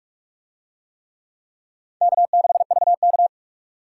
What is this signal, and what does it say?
Morse code, heard as a keyed tone, about 44 wpm, 700 Hz. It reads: K6VK